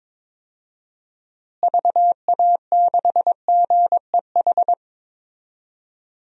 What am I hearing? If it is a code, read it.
Morse code, heard as a keyed tone, about 21 wpm, 700 Hz. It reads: VA6GEH
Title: DXCC-100 Call Signs - 6 Characters - Repeated 3x - 22wpm